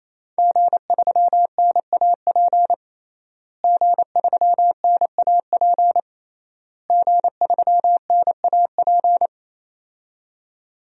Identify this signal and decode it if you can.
Morse code, heard as a keyed tone, about 28 wpm, 700 Hz. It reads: G3NAP G3NAP G3NAP
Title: DXCC-100 Call Signs - 5 Characters - Repeated 3x - 28wpm